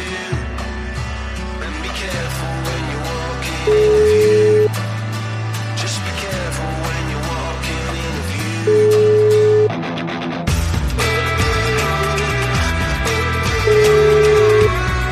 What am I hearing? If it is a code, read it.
Ringback tone (Europe, 425 Hz), with noise.